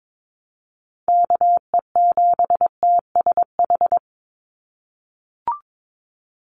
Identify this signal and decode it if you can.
Morse code, heard as a keyed tone, about 22 wpm, 700 Hz. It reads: KE7TSH
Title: USA Call Signs 22wpm